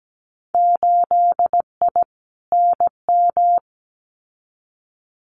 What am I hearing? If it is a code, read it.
Morse code, heard as a keyed tone, about 17 wpm, 700 Hz. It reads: 8I NM
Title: Winter Field Day Exchanges Encore 17wpm